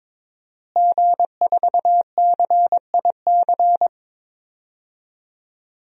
Morse code, heard as a keyed tone, about 22 wpm, 700 Hz. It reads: G4CIC
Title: DXCC-100 Call Signs - 5 Characters - Repeated 3x - 22wpm